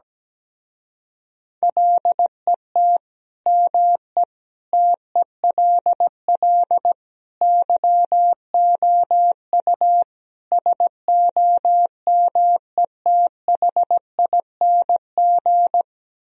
Morse code, heard as a keyed tone, about 17 wpm, 700 Hz. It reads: LET ME TELL YOU SOMETHING
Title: Speed-Racing - Sets of 5 Words 17wpm